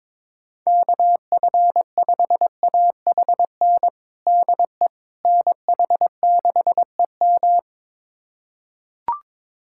Morse code, heard as a keyed tone, about 22 wpm, 700 Hz. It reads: KF5AHN DE NH6EM
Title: USA Callsign DE Callsign 22wpm